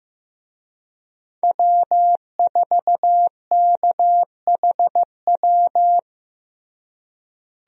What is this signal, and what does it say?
Morse code, heard as a keyed tone, about 16 wpm, 700 Hz. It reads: W4KHW
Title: USA Call Signs - 5 Characters - Repeated 2x - 15wpm